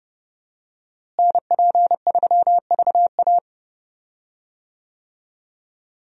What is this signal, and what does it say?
Morse code, heard as a keyed tone, about 30 wpm, 700 Hz. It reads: NP3VA
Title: USA Call Signs - 5 Characters - Repeated 2x - 30wpm